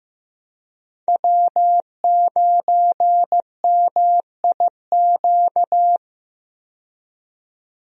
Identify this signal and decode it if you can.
Morse code, heard as a keyed tone, about 16 wpm, 700 Hz. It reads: W9MIQ
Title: USA Call Signs - 5 Characters - 15wpm